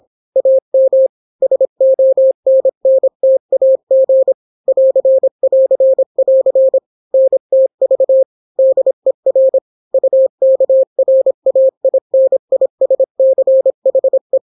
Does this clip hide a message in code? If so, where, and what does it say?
Yes: Morse code from 0.4 to 14.4 s, heard as a keyed tone, about 26 wpm, 530 Hz. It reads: AM SONNTAG <AR><AR><AR> NTV DER UKRAINISCHE